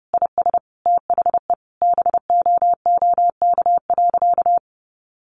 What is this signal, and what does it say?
Morse code, heard as a keyed tone, about 30 wpm, 700 Hz. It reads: IS THE BOOK.
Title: Sentences from Top 300 Words 30wpm